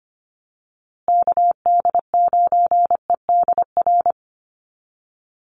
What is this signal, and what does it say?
Morse code, heard as a keyed tone, about 25 wpm, 700 Hz. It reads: KD9EDR
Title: USA Call Signs Repeated 2x 25wpm